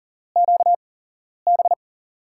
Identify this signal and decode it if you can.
Morse code, heard as a keyed tone, about 40 wpm, spaced as for 11 wpm, 700 Hz. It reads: QB
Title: USA Call Signs - 6 Characters - 8x Character Spacing - 40wpm